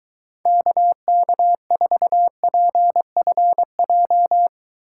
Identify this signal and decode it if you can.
Morse code, heard as a keyed tone, about 23 wpm, 700 Hz. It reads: KK4PFJ